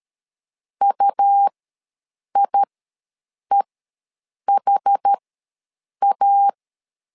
Morse code, heard as a keyed tone, about 13 wpm, spaced as for 7 wpm, 800 Hz. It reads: UIEHA
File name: LEZIONI / 2 UHDN / 2 UHD ws 12wpm 1722